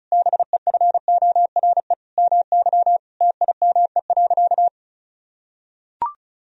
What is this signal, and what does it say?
Morse code, heard as a keyed tone, about 36 wpm, 700 Hz. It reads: BEFORE MY TIME.
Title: Sentences from Top 1000 Words - Review Part 1 - 35wpm